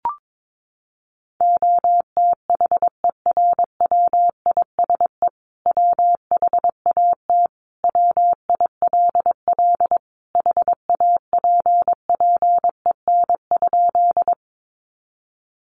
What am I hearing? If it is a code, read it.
Morse code, heard as a keyed tone, about 22 wpm, 700 Hz. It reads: OTHERWISE WHAT WILL HAPPEN?
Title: Sentences from Top 1200 Words 22wpm